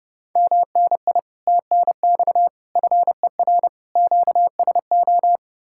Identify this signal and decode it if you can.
Morse code, heard as a keyed tone, about 40 wpm, spaced as for 35 wpm, 700 Hz. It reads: MNI TNX FER QSO
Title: Speed-Racing Common QSO Phrases - 30wpm